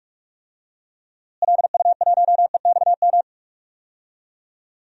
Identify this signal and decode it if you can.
Morse code, heard as a keyed tone, about 46 wpm, 700 Hz. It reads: LU1EXM
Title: DXCC-100 Call Signs - 6 Characters - Repeated 3x - 45wpm